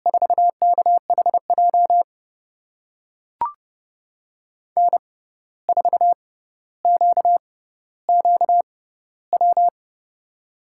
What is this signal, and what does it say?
Morse code, heard as a keyed tone, about 32 wpm, spaced as for 15 wpm, 700 Hz. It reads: 4KHJ N4QQW
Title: USA Call Signs - 6x Character Spacing - 30wpm